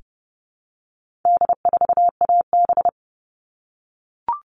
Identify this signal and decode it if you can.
Morse code, heard as a keyed tone, about 30 wpm, 700 Hz. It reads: D4AB
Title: All Cut Numbers - 4-Digit 30wpm